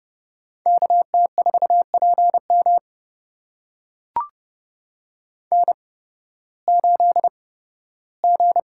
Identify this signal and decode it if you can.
Morse code, heard as a keyed tone, about 31 wpm, spaced as for 16 wpm, 700 Hz. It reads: KT4PM N8G
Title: USA Call Signs - 5 Characters - 8x Character Spacing - 30wpm